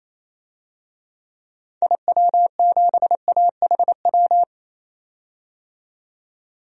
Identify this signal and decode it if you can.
Morse code, heard as a keyed tone, about 28 wpm, 700 Hz. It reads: IW7AHW
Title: DXCC-100 Call Signs - 6 Characters - 28wpm